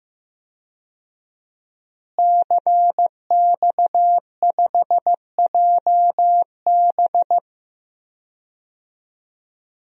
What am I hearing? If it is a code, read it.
Morse code, heard as a keyed tone, about 15 wpm, 700 Hz. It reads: CX5JB
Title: DXCC-100 Call Signs - 5 Characters - Repeated 3x - 15wpm